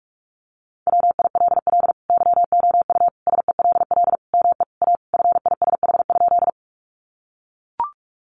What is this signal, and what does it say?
Morse code, heard as a keyed tone, about 44 wpm, 700 Hz. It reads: WILL YOU SELL ME A FISH?
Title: Sentences from Top 1100 Words 45wpm